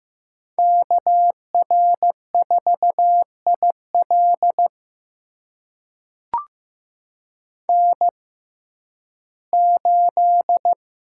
Morse code, heard as a keyed tone, about 15 wpm, spaced as for 11 wpm, 700 Hz. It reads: KR4IL N8
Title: USA Call Signs - 5 Characters - 6x Character Spacing - 15wpm